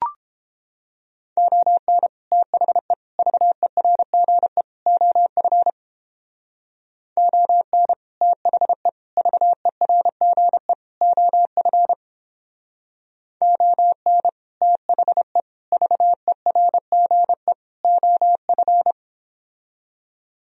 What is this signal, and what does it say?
Morse code, heard as a keyed tone, about 31 wpm, 700 Hz. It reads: ON THE VERGE OF ON THE VERGE OF ON THE VERGE OF